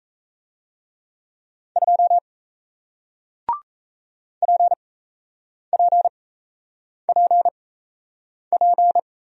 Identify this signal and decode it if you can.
Morse code, heard as a keyed tone, about 37 wpm, spaced as for 8 wpm, 700 Hz. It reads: 2 PPPP